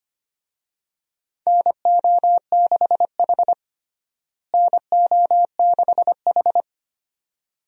Morse code, heard as a keyed tone, about 25 wpm, 700 Hz. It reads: NO6H NO6H
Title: USA Call Signs - 4 Characters - Repeated 2x - 25wpm